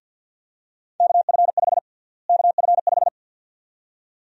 Morse code, heard as a keyed tone, about 49 wpm, 700 Hz. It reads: KF5 KF5
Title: USA Call Sign Prefixes - Repeated 2x - 50wpm